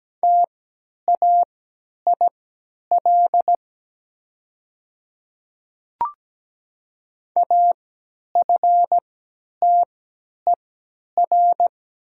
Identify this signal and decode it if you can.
Morse code, heard as a keyed tone, about 17 wpm, spaced as for 10 wpm, 700 Hz. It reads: TAIL AFTER
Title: 3x Character Spacing - Top 500 Words - 17wpm